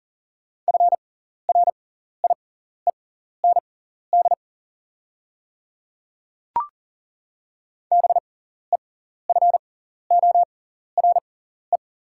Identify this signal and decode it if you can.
Morse code, heard as a keyed tone, about 40 wpm, spaced as for 14 wpm, 700 Hz. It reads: FRIEND BEFORE
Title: 6x Character Spacing - Top 300 Words - 40wpm